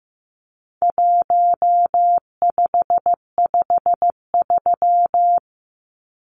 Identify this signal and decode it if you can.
Morse code, heard as a keyed tone, about 15 wpm, 700 Hz. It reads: 1553
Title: Numbers - 4-Digit 15wpm